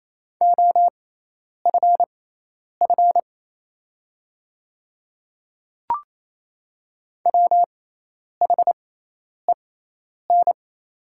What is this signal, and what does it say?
Morse code, heard as a keyed tone, about 28 wpm, spaced as for 10 wpm, 700 Hz. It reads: OFF WHEN
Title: 6x Character Spacing - Top 300 Words - 28wpm